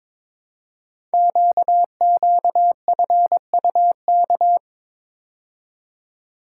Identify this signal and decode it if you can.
Morse code, heard as a keyed tone, about 22 wpm, 700 Hz. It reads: QQFUK
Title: Code Groups - 5 Letters - 22wpm